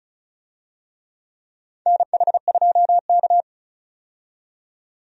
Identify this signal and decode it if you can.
Morse code, heard as a keyed tone, about 34 wpm, 700 Hz. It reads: NH2K
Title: USA Call Signs - 4 Characters - 35wpm